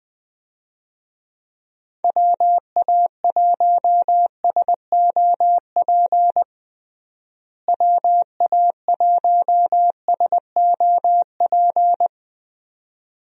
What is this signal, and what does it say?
Morse code, heard as a keyed tone, about 20 wpm, 700 Hz. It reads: WA1SOP WA1SOP
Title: USA Call Signs - 6 Characters - Repeated 2x - 20wpm